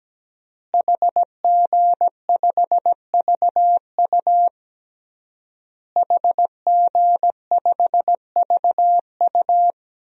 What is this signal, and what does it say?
Morse code, heard as a keyed tone, about 17 wpm, 700 Hz. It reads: HG5VU HG5VU
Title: DXCC-100 Call Signs - 5 Characters - Repeated 3x - 17wpm